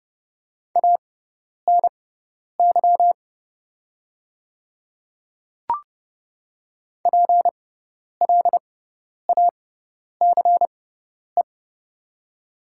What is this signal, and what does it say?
Morse code, heard as a keyed tone, about 30 wpm, spaced as for 10 wpm, 700 Hz. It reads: ANY PLACE
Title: 6x Character Spacing - Top 300 Words - 30wpm